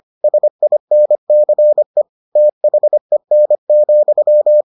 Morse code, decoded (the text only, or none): SINCE THEN,